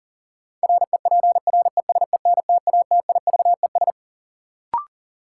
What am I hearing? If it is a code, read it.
Morse code, heard as a keyed tone, about 37 wpm, 700 Hz. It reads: REPRESENTATIVES
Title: Sets of 5 Words Encore - New Word Drill - 40wpm